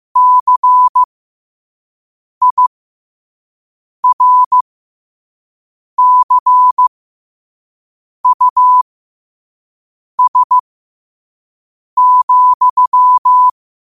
Morse code, heard as a keyed tone, about 15 wpm, spaced as for 5 wpm, 1000 Hz. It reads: CIRCUS,
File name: qotdfiles / 5wpm